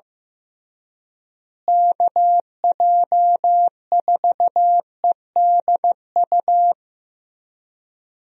Morse code, heard as a keyed tone, about 15 wpm, 700 Hz. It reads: KJ4EDU
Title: USA Call Signs - 6 Characters - Repeated 3x - 15wpm